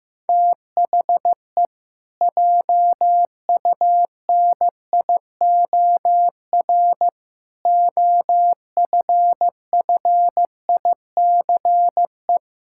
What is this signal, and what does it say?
Morse code, heard as a keyed tone, about 15 wpm, 700 Hz. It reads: THE JUNIOR OFFICE
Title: Pangram Challenge v2 - 15wpm